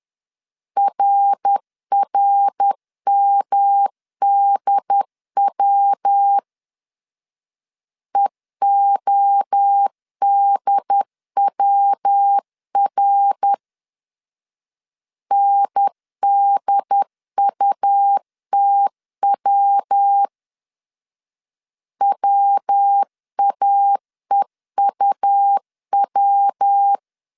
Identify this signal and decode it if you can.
Morse code, heard as a keyed tone, about 10 wpm, 800 Hz. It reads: RRMDW EODWR NDUTW WAEUW
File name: CW / DET / 1 3 1500C S B 10wpm 3335